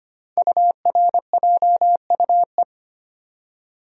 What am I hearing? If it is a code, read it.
Morse code, heard as a keyed tone, about 25 wpm, 690 Hz. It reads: URJUE
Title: Code Groups - 5 Letters - 25wpm